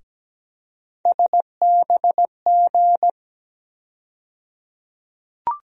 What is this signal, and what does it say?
Morse code, heard as a keyed tone, about 17 wpm, 700 Hz. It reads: SBG
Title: USA Call Sign Postfixes - 3 Letters 17wpm